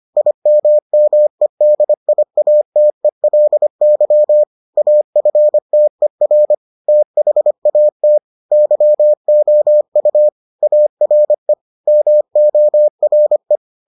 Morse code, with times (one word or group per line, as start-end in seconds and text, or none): 0.2-4.4 IMMEDIATELY
4.8-6.5 AFTER
6.9-8.2 THAT
8.5-10.3 YOU
10.6-11.5 ARE
11.9-13.6 MORE